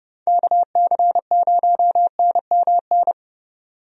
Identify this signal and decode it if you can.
Morse code, heard as a keyed tone, about 32 wpm, 700 Hz. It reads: KC0NMN
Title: USA Call Signs - 6 Characters - 30wpm